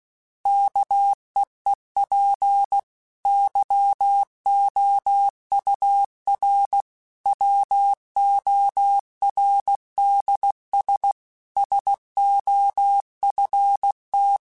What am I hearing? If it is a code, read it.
Morse code, heard as a keyed tone, about 16 wpm, 800 Hz. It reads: KEEP YOUR WORDS SOFT
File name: Chuckles / Chuckles-mp3s / SoftWords @15